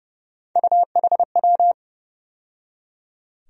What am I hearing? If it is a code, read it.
Morse code, heard as a keyed tone, about 31 wpm, 700 Hz. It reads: UHW
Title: USA Call Sign Postfixes - 3 Letters 30wpm